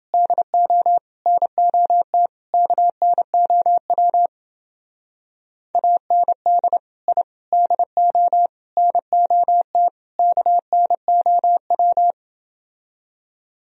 Morse code, heard as a keyed tone, about 29 wpm, 700 Hz. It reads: DO NOT KNOW AND I DO NOT KNOW